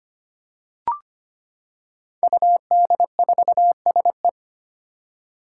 Morse code, heard as a keyed tone, about 25 wpm, 700 Hz. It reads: UD4SE
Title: Five Challenging Letter-Number Combinations 25wpm